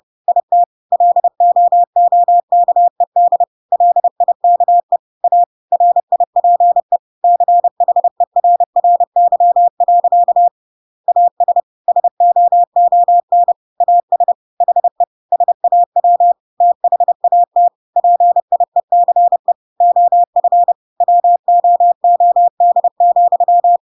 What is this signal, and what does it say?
Morse code, heard as a keyed tone, about 30 wpm, 700 Hz. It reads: IT LOOKED LIKE A RIPE CHERRY. AS SOON AS HE SAW THAT PIECE OF WOOD,
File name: Pinocchio / Pinocchio ch1 30 30 ik1hge